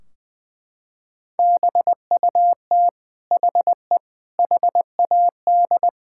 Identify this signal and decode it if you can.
Morse code, heard as a keyed tone, about 20 wpm, 700 Hz. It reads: BUT HE HAD
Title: Sentences from Top 1600 Words 20wpm